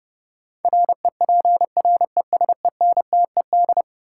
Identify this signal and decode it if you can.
Morse code, heard as a keyed tone, about 30 wpm, 700 Hz. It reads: REPRESENTED
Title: Sets of 5 Words - New Word Drill - 30wpm